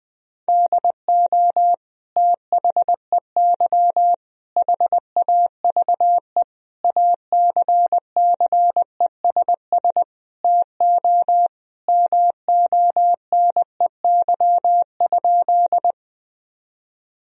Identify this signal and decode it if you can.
Morse code, heard as a keyed tone, about 20 wpm, 700 Hz. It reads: DO THEY HAVE ACCESS TO MONEY?
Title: Sentences from Top 1000 Words - Review Part 4 - 20wpm